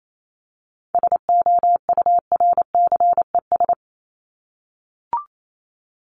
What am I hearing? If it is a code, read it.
Morse code, heard as a keyed tone, about 28 wpm, 700 Hz. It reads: SOURCES
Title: ES:S Suffix - Top 100 Words with Base Words 28wpm